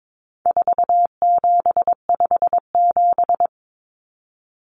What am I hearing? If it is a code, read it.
Morse code, heard as a keyed tone, about 22 wpm, 700 Hz. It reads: 4757
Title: Numbers - 4-Digit 22wpm